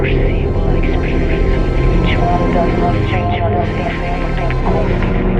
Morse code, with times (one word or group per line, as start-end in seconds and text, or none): none